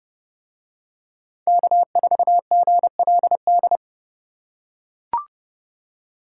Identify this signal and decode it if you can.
Morse code, heard as a keyed tone, about 29 wpm, 700 Hz. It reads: K4GLD